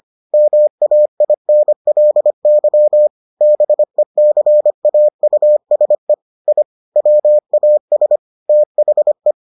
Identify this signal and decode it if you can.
Morse code, heard as a keyed tone, about 25 wpm, 600 Hz. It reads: MAINLY BECAUSE I WAS THE